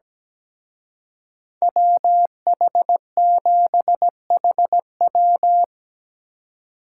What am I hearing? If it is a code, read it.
Morse code, heard as a keyed tone, about 19 wpm, 700 Hz. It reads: WH7HW
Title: Speed Racing - USA 2x2 Call Signs 17wpm